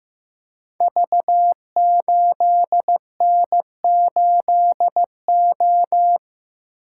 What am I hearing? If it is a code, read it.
Morse code, heard as a keyed tone, about 15 wpm, 700 Hz. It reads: V8N8O
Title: Code Groups - 5 Letter:Numbers - 15wpm